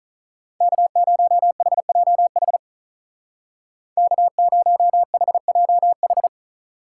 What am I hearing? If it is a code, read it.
Morse code, heard as a keyed tone, about 43 wpm, 700 Hz. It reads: K0HJH K0HJH
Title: Speed Racing - USA 1x3 Call Signs 35wpm